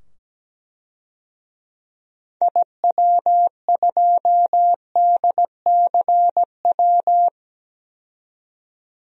Morse code, heard as a keyed tone, about 17 wpm, 700 Hz. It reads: IW2DCW
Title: DXCC-100 Call Signs - 6 Characters - 17wpm